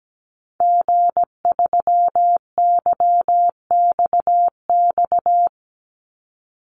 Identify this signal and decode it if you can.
Morse code, heard as a keyed tone, about 17 wpm, 700 Hz. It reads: G3YXX
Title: Call Signs - 17wpm